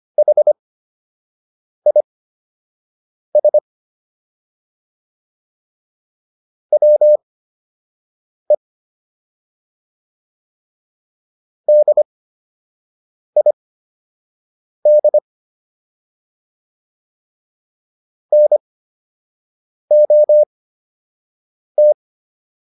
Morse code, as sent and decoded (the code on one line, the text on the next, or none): .... .. ... / .-- . / -.. .. -.. / -. --- -
HIS WE DID NOT